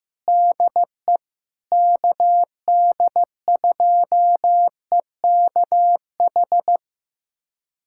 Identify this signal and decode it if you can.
Morse code, heard as a keyed tone, about 15 wpm, 700 Hz. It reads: DE KD2EKH
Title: USA Callsign DE Callsign 15wpm